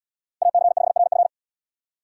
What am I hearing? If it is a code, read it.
Morse code, heard as a keyed tone, about 77 wpm, 700 Hz. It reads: AC5RF